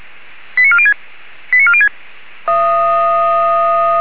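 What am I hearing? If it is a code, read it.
Five-tone selective call (ZVEI), heard as quick runs of five tones, tones of about 70 ms. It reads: 89406 89406
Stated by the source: ZVEI five-tone call 89360 received on SDR.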